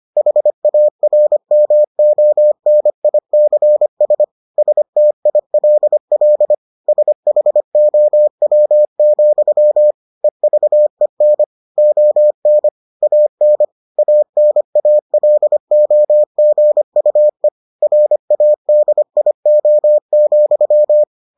Morse code, heard as a keyed tone, about 25 wpm, 600 Hz. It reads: HARMONICS STILL SHOW, EVEN ON AN ANALOGUE RADIO,